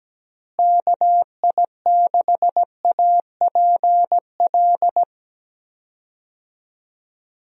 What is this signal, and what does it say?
Morse code, heard as a keyed tone, about 17 wpm, 700 Hz. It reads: KI6APL
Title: USA Call Signs - 6 Characters - Repeated 3x - 17wpm